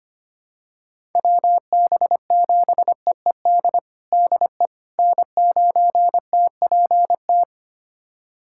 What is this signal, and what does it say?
Morse code, heard as a keyed tone, about 25 wpm, 700 Hz. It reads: WB7EED DE N9TPT
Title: USA Callsign DE Callsign 25wpm